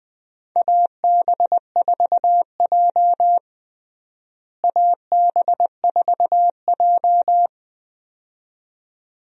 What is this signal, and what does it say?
Morse code, heard as a keyed tone, about 20 wpm, 700 Hz. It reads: AB4J AB4J
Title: USA Call Signs - 4 Characters - Repeated 2x - 20wpm